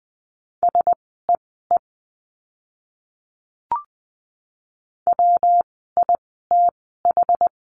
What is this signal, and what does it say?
Morse code, heard as a keyed tone, about 20 wpm, spaced as for 15 wpm, 700 Hz. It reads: SEE WITH
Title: 139 Lesson 23f - Letter f - Words 20wpm